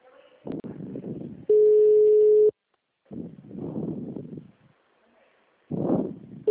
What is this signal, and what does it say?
Ringback tone (Europe, 425 Hz).